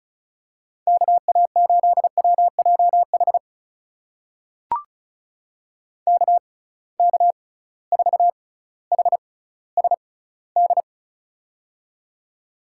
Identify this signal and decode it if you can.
Morse code, heard as a keyed tone, about 36 wpm, spaced as for 18 wpm, 700 Hz. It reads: KA8WJH KK4HSD
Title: USA Call Signs - 6 Characters - 6x Character Spacing - 35wpm